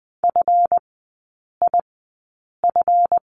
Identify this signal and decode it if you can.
Morse code, heard as a keyed tone, about 20 wpm, spaced as for 8 wpm, 700 Hz. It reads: FIF
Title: 138 Lesson 23e - Letter f - Three Characters 20wpm